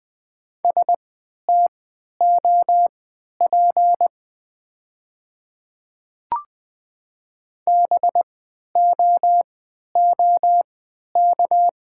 Morse code, heard as a keyed tone, about 20 wpm, spaced as for 11 wpm, 700 Hz. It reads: STOP BOOK
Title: 3x Character Spacing - Top 300 Words - 20wpm